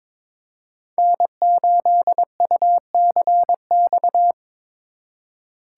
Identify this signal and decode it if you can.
Morse code, heard as a keyed tone, about 22 wpm, 700 Hz. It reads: N8UCX
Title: USA Call Signs - 5 Characters - 2x Character Spacing - 22wpm